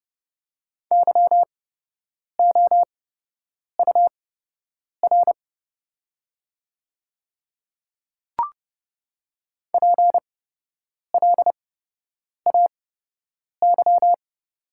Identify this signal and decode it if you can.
Morse code, heard as a keyed tone, about 30 wpm, spaced as for 8 wpm, 700 Hz. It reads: YOUR PLAY
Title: 8x Character Spacing - Top 300 Words - 30wpm